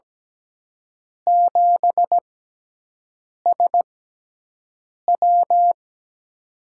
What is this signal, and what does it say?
Morse code, heard as a keyed tone, about 17 wpm, spaced as for 6 wpm, 700 Hz. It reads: 7SW
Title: USA Call Signs - 6x Character Spacing - 17wpm